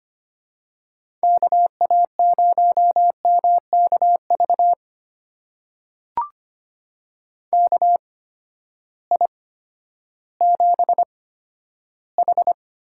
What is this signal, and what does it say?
Morse code, heard as a keyed tone, about 25 wpm, spaced as for 12 wpm, 700 Hz. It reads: KA0MKV KI7H
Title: USA Call Signs - 6 Characters - 8x Character Spacing - 25wpm